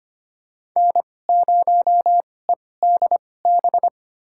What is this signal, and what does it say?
Morse code, heard as a keyed tone, about 25 wpm, spaced as for 18 wpm, 700 Hz. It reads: N0EDB
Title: USA Call Signs - 2x Character Spacing - 25wpm